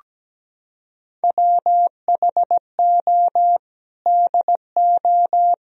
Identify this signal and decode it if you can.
Morse code, heard as a keyed tone, about 17 wpm, 700 Hz. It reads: WHO DO
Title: Sentences from Top 1000 Words - Review Part 4 - 17wpm